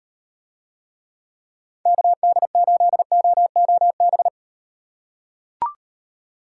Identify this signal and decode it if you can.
Morse code, heard as a keyed tone, about 35 wpm, 700 Hz. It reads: KD8OOB